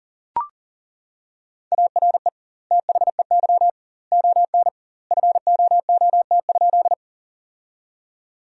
Morse code, heard as a keyed tone, about 40 wpm, 700 Hz. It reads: ARE THEY ON FOOT?
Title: 2x Word-Spacing - Sentences from Top 500 Words - 40wpm